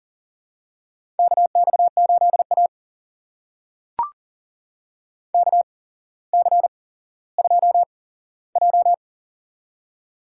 Morse code, heard as a keyed tone, about 40 wpm, spaced as for 17 wpm, 700 Hz. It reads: KX8A KC2J
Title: USA Call Signs - 4 Characters - 8x Character Spacing - 40wpm